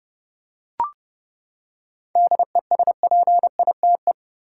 Morse code, heard as a keyed tone, about 30 wpm, 700 Hz. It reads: DESPITE
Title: Seven Letter Words - 30wpm